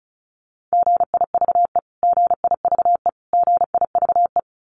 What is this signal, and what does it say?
Morse code, heard as a keyed tone, about 34 wpm, 700 Hz. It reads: GIVE GIVE GIVE
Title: Top 300 Words Repeated 35wpm